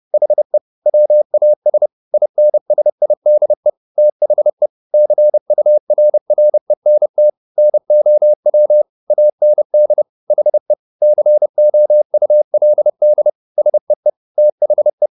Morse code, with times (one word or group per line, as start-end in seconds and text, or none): 0.1-0.6 HE
0.9-1.9 WAS
2.1-3.7 INSIDE
4.0-4.7 THE
4.9-7.3 CURRENT
7.6-8.8 NOW
9.1-10.0 AND
10.3-10.7 HE
11.0-13.3 COULD
13.6-14.1 SEE
14.4-15.1 THE